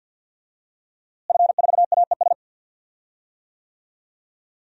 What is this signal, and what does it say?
Morse code, heard as a keyed tone, about 51 wpm, 700 Hz. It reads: F4AES
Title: DXCC-100 Call Signs - 5 Characters - Repeated 2x - 50wpm